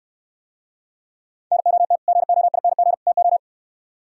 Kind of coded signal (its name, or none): Morse code